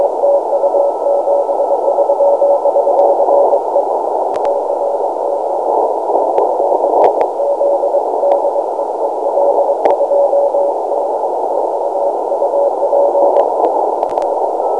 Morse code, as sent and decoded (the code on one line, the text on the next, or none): none